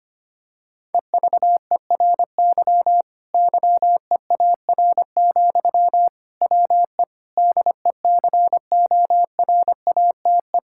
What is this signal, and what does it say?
Morse code, heard as a keyed tone, about 25 wpm, 700 Hz. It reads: EVERY YEAR, WE DECORATE
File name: practice / 25wpm / Difficult Sentences with Hints - 25wpm